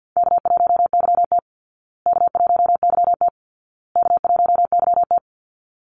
Morse code, heard as a keyed tone, about 53 wpm, 700 Hz. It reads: K1YT K1YT K1YT
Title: Easy Call Signs Repeated 3x - 50wpm